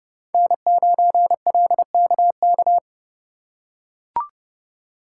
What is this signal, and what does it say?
Morse code, heard as a keyed tone, about 30 wpm, 700 Hz. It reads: N9LKK